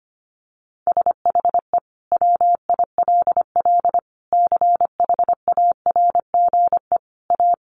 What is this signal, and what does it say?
Morse code, heard as a keyed tone, about 25 wpm, 700 Hz. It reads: SHE WILL CHARGE A
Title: Sentences from Top 1500 Words 25wpm